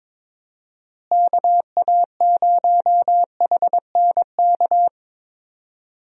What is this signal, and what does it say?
Morse code, heard as a keyed tone, about 22 wpm, 700 Hz. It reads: KA0HNK
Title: USA Call Signs - 6 Characters - Repeated 3x - 22wpm